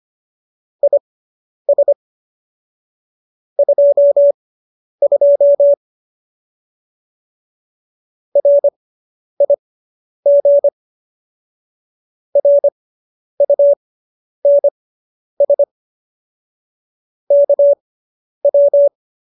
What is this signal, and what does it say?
Morse code, heard as a keyed tone, about 25 wpm, spaced as for 10 wpm, 570 Hz. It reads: IS 22 RIG RUNS KW
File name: QSO220 10